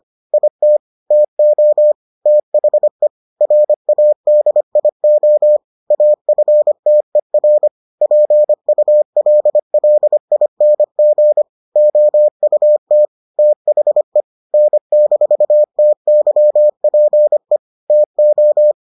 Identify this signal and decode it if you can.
Morse code, heard as a keyed tone, about 25 wpm, 600 Hz. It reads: IT TO THE RADIO AFTER PULLING OUT THE N-TYPE TO